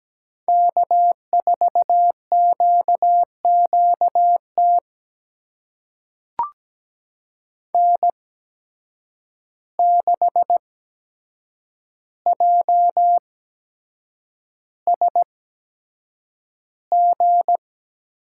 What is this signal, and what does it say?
Morse code, heard as a keyed tone, about 17 wpm, spaced as for 7 wpm, 700 Hz. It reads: K4QQT N6JSG
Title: USA Call Signs - 5 Characters - 8x Character Spacing - 17wpm